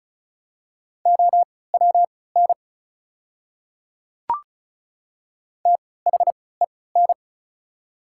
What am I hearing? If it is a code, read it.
Morse code, heard as a keyed tone, about 34 wpm, spaced as for 20 wpm, 700 Hz. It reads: OWN THEN
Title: 3x Character Spacing - Top 300 Words - 35wpm